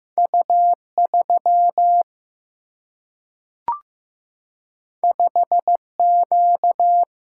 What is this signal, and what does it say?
Morse code, heard as a keyed tone, about 15 wpm, 700 Hz. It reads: U 3 5 Q
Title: Single-Letter Single-Number 15wpm